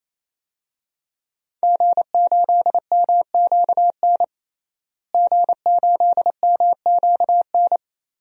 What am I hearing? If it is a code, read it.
Morse code, heard as a keyed tone, about 27 wpm, 700 Hz. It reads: G8MQN G8MQN
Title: DXCC-100 Call Signs - 5 Characters - Repeated 3x - 28wpm